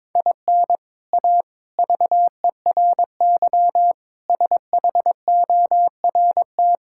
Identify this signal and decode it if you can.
Morse code, heard as a keyed tone, about 22 wpm, 700 Hz. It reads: IN A VERY SHORT
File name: practice / 22wpm / Sets of 7 Words - 22wpm